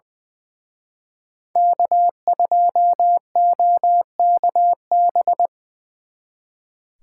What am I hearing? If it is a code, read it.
Morse code, heard as a keyed tone, about 23 wpm, 700 Hz. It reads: K2OKB
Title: Speed Racing - USA 1x3 Call Signs 20wpm